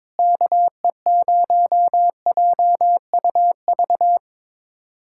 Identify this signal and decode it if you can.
Morse code, heard as a keyed tone, about 22 wpm, 700 Hz. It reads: KE0JUV
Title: USA Callsign DE Callsign 22wpm